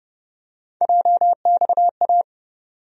Morse code, heard as a keyed tone, about 31 wpm, 700 Hz. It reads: JXA
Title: USA Call Sign Postfixes - 3 Letters 30wpm